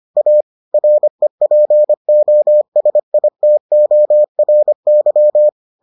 Morse code, heard as a keyed tone, about 25 wpm, 600 Hz. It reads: A REPOSITORY